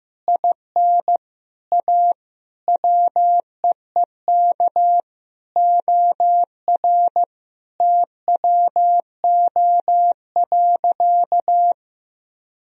Morse code, heard as a keyed tone, about 15 wpm, 700 Hz. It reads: IN A WEEK OR TWO.
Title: Sentences from Top 1000 Words - Review Part 2 - 15wpm